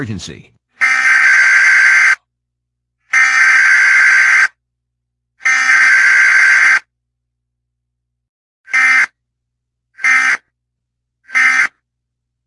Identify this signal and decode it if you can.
Emergency Alert System (SAME) tones (header bursts), heard as harsh buzzing bursts, with speech. It is heard from 0.8 to 11.7 s.